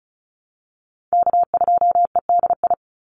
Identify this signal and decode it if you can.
Morse code, heard as a keyed tone, about 34 wpm, 700 Hz. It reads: K2EDI
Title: USA Call Signs 35wpm